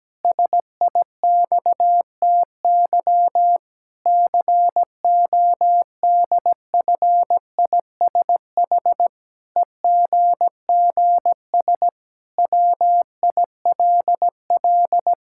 Morse code, heard as a keyed tone, about 17 wpm, 700 Hz. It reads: SIXTY CODFISH EGGS WILL
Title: Pangram Challenge v2 - 17wpm